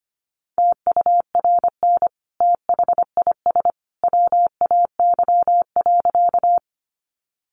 Morse code, heard as a keyed tone, about 25 wpm, 700 Hz. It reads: TURN THIS WAY.
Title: Sentences from Top 200 Words 25wpm